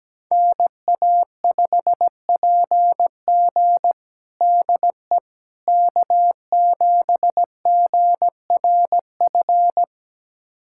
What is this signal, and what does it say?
Morse code, heard as a keyed tone, about 17 wpm, 700 Hz. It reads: NA5PG DE K7GRF
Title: USA Callsign DE Callsign - 5 Characters - 17wpm